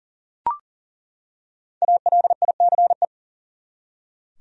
Morse code, heard as a keyed tone, about 39 wpm, 700 Hz. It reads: ALICE
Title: USA Common Names - Max 5 Letters - 40wpm